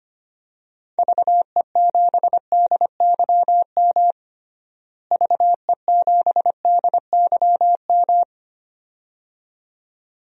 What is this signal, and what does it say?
Morse code, heard as a keyed tone, about 25 wpm, 700 Hz. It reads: VE7DYM VE7DYM
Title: DXCC-100 Call Signs - 6 Characters - Repeated 2x - 25wpm